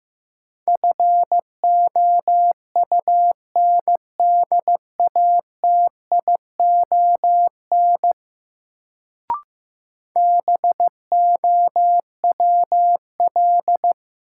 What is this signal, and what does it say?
Morse code, heard as a keyed tone, about 15 wpm, 700 Hz. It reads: FOUNDATION BOWL
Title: Sentences from Top 1200 Words - New Word Drill - 15wpm